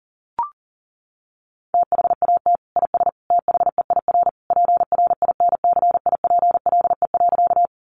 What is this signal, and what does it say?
Morse code, heard as a keyed tone, about 40 wpm, 700 Hz. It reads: THAT IS THEIR PRINCIPLE.
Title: Sentences from Top 1400 Words 40wpm